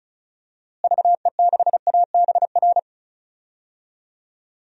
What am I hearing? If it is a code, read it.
Morse code, heard as a keyed tone, about 34 wpm, 700 Hz. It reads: VE6ABR
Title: DXCC-100 Call Signs - 6 Characters - Repeated 2x - 35wpm